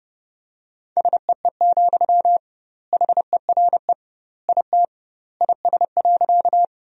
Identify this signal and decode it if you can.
Morse code, heard as a keyed tone, about 30 wpm, 700 Hz. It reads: SEE, HERE IT IS.
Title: 2x Word-Spacing - Sentences from Top 200 Words - 30wpm